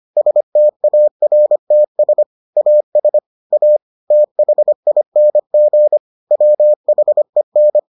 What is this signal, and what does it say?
Morse code, heard as a keyed tone, about 25 wpm, 600 Hz. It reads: STARTS AS A THING WHEN